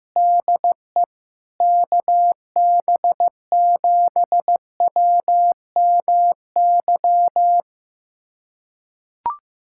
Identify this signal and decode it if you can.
Morse code, heard as a keyed tone, about 15 wpm, 700 Hz. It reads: DE KB7WMY
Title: USA Callsign DE Callsign - 6 Characters - 15wpm